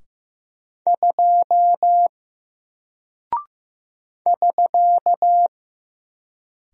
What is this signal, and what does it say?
Morse code, heard as a keyed tone, about 15 wpm, 700 Hz. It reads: 2 <SK>
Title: Single Character (Letter:Number:Punctuation:Prosign) Alt 15wpm